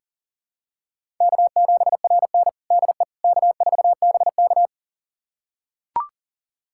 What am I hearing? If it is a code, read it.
Morse code, heard as a keyed tone, about 40 wpm, 700 Hz. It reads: K7RN DE K4BK
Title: USA Callsign DE Callsign - 4 Characters - 40wpm